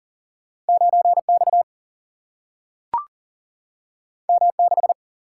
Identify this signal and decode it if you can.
Morse code, heard as a keyed tone, about 40 wpm, 700 Hz. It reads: 9X M6
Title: Single-Letter Single-Number 40wpm